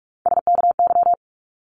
Morse code, heard as a keyed tone, about 44 wpm, 700 Hz. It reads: SKY
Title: Top 1001-1250 Words 45wpm